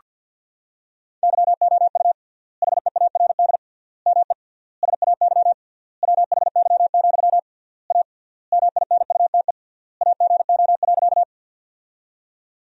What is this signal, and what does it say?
Morse code, heard as a keyed tone, about 49 wpm, 700 Hz. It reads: YOU HEARD ME SAY WHY, A MINUTE AGO.
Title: 3x Word-Spacing - Sentences from Top 500 Words - 50wpm